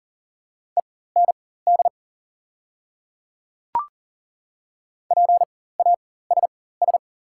Morse code, heard as a keyed tone, about 40 wpm, spaced as for 19 wpm, 690 Hz. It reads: END PASS
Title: 4x Character Spacing - Top 500 Words - 40wpm